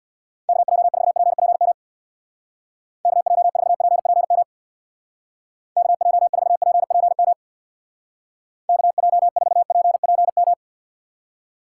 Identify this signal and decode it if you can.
Morse code, heard as a keyed tone, about 61 wpm, 700 Hz. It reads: KJ4PPM KJ4PPM KJ4PPM KJ4PPM
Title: Speed Racing - USA Call Signs 50wpm